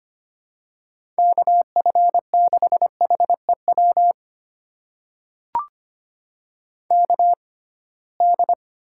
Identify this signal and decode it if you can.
Morse code, heard as a keyed tone, about 25 wpm, spaced as for 19 wpm, 700 Hz. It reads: KF6HEW KD
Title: USA Call Signs - 6 Characters - 6x Character Spacing - 25wpm